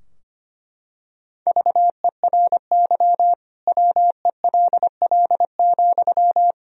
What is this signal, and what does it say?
Morse code, heard as a keyed tone, about 25 wpm, 700 Hz. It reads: VERY WELL,
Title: Sentences from Top 1000 Words - Review Part 4 - 25wpm